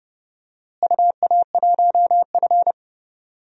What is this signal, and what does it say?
Morse code, heard as a keyed tone, about 30 wpm, 700 Hz. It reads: UA1F
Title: DXCC-100 Call Signs - 4 Characters - Repeated 3x - 30wpm